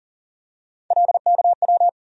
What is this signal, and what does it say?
Morse code, heard as a keyed tone, about 40 wpm, 700 Hz. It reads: LKW
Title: USA Call Sign Postfixes - 3 Letters 40wpm